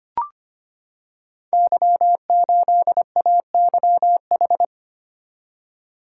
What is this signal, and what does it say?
Morse code, heard as a keyed tone, about 25 wpm, 700 Hz. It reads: Y8AYH
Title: Code Groups - 5 Letter:Numbers - 25wpm